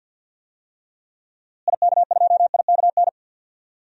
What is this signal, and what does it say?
Morse code, heard as a keyed tone, about 52 wpm, 700 Hz. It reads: IK2IKN
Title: DXCC-100 Call Signs - 6 Characters - Repeated 3x - 50wpm